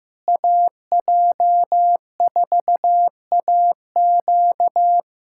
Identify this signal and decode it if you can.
Morse code, heard as a keyed tone, about 15 wpm, 700 Hz. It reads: AJ4AQ
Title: USA Callsign DE Callsign 15wpm